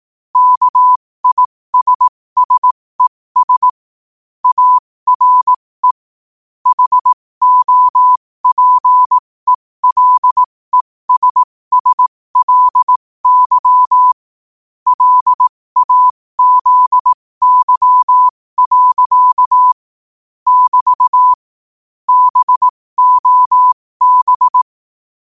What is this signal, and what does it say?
Morse code, heard as a keyed tone, about 18 wpm, spaced as for 15 wpm, 1000 Hz. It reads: KISSES ARE HOPELESSLY LAZY. <BT> BOB
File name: qotdfiles / 15wpm